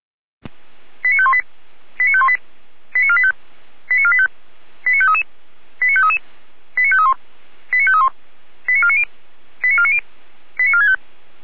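Five-tone selective call (ZVEI), heard as quick runs of five tones, tones of about 70 ms. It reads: 89518 89518 89485 89485 89533 89533 89531 89531 89490 89490 89465
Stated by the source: ZVEI five-tone call received by SDR.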